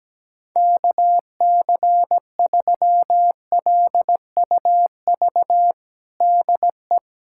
Morse code, heard as a keyed tone, about 17 wpm, 700 Hz. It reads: KC3LUV DE
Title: USA Callsign DE Callsign 17wpm